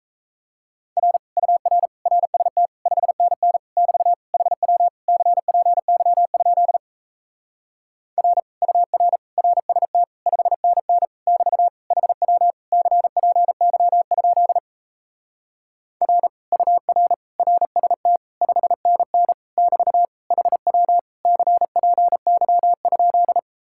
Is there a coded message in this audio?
Yes: Morse code from 1.0 to 23.4 s, heard as a keyed tone, about 37 wpm, 700 Hz. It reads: R UR RST 5NN <BT> HW CPY? R UR RST 5NN <BT> HW CPY? R UR RST 5NN <BT> HW CPY?